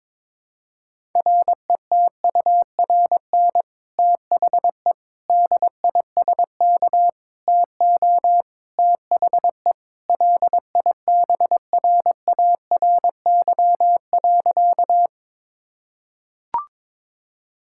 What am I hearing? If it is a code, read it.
Morse code, heard as a keyed tone, about 21 wpm, 700 Hz. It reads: RETURN THE DISK TO THE LIBRARY.
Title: Sentences from Top 1000 Words - Review Part 6 - 22wpm